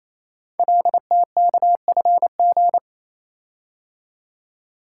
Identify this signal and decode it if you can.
Morse code, heard as a keyed tone, about 27 wpm, 700 Hz. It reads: LTKFG
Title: Code Groups - 5 Letters - 28wpm